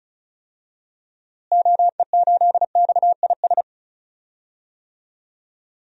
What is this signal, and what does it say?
Morse code, heard as a keyed tone, about 34 wpm, 700 Hz. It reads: OE8XIS
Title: DXCC-100 Call Signs - 6 Characters - Repeated 3x - 35wpm